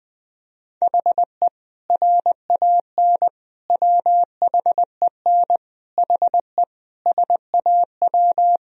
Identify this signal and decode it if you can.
Morse code, heard as a keyed tone, about 21 wpm, 700 Hz. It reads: HE RAN WHEN HE SAW
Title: Sentences from Top 500 Words and Abbreviations 20wpm